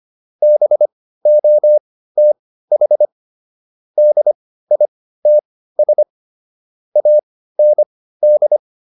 Morse code, heard as a keyed tone, about 25 wpm, spaced as for 15 wpm, 600 Hz. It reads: BOTH DITS AND
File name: SS215 15